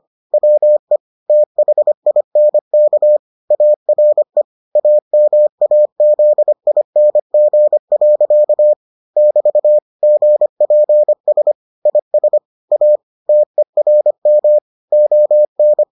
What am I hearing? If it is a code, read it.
Morse code, heard as a keyed tone, about 25 wpm, 600 Hz. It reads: WE THINK ARE AMAZING. <BT> GPS IS A TERM ON